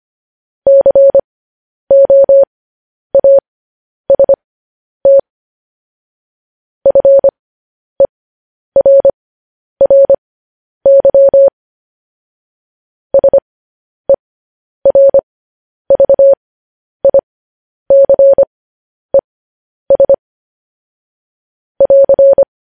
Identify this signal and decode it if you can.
Morse code, heard as a keyed tone, about 25 wpm, spaced as for 10 wpm, 550 Hz. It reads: COAST FERRY SERVICES <AR>
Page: news-headlines-in-morse-code-at-10-wpm